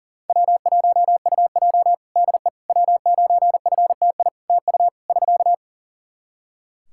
Morse code, heard as a keyed tone, about 43 wpm, 700 Hz. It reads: W1UJ DE W9FTI TU <SK>
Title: Practice QSO - Exchange 4 - Ending - 40wpm